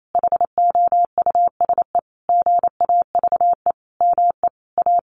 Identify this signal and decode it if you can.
Morse code, heard as a keyed tone, about 28 wpm, 700 Hz. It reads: HOUSE GAVE ME A
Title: Sentences from Top 1500 Words 28wpm